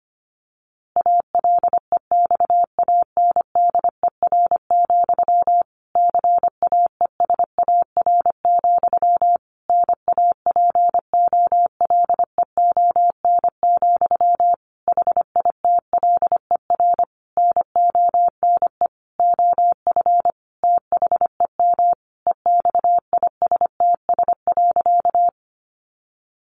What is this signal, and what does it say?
Morse code, heard as a keyed tone, about 25 wpm, 700 Hz. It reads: ALEXANDER, CAESAR, NAPOLEON, HITLER NONE OF THEM EXISTS.
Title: A Sound Of Thunder 25wpm